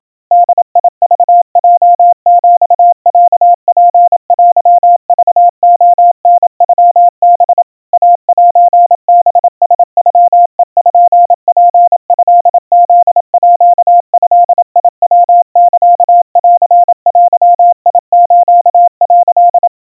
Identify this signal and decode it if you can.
Morse code (Japanese Wabun code), heard as a keyed tone, about 27 wpm, 700 Hz. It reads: ボクヲヒロツテクレタノハ イセハラノペツトフードヤサンデス」
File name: kawa / Morse sample / kamyu 100 cpm 700 Hz Crispy